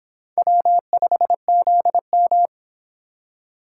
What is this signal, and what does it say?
Morse code, heard as a keyed tone, about 24 wpm, 700 Hz. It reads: W5ZM